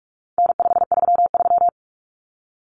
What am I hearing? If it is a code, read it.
Morse code, heard as a keyed tone, about 44 wpm, 700 Hz. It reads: N533
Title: Common Cut Numbers - 4-Digit 45wpm